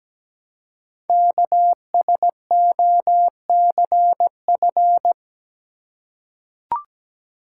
Morse code, heard as a keyed tone, about 17 wpm, 700 Hz. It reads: KSOCF
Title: Code Groups - 5 Letter:Numbers - 17wpm